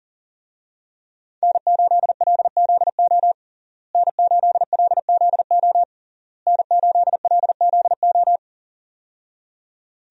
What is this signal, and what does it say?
Morse code, heard as a keyed tone, about 40 wpm, 700 Hz. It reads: N8LZO N8LZO N8LZO
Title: USA Call Signs - 5 Characters - Repeated 3x - 40wpm